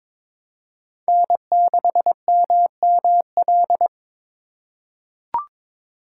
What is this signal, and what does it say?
Morse code, heard as a keyed tone, about 22 wpm, 700 Hz. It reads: N6MML
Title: USA Call Signs - 5 Characters - 2x Character Spacing - 22wpm